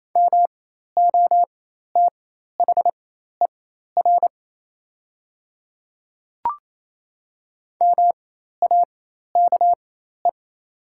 Morse code, heard as a keyed tone, about 28 wpm, spaced as for 13 wpm, 700 Hz. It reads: MOTHER MAKE
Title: 4x Character Spacing - Top 200 Words - 28wpm